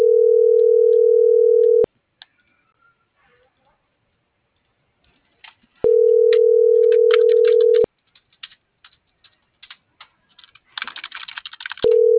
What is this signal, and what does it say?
Ringback tone (North America), 440 + 480 Hz.